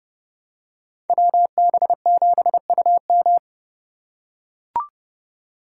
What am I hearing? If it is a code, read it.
Morse code, heard as a keyed tone, about 30 wpm, 700 Hz. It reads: WB7UM